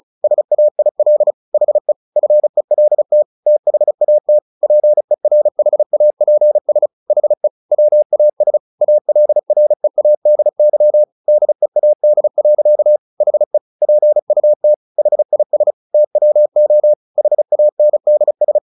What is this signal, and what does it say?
Morse code, heard as a keyed tone, about 34 wpm, 600 Hz. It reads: SAIL HE FELT THAT PERHAPS HE WAS ALREADY DEAD. HE PUT HIS TWO HANDS
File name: the old man and the sea 2 35wpm